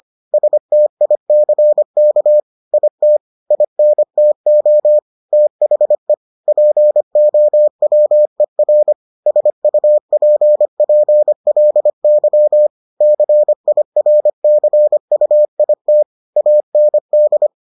Morse code, heard as a keyed tone, about 25 wpm, 600 Hz. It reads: STICK IT INTO THE POWER SUPPLY CIRCUIT AND